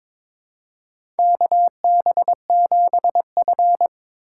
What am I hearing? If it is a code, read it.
Morse code, heard as a keyed tone, about 22 wpm, 700 Hz. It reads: KB7F
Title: USA Call Signs - 3x Character Spacing - 22wpm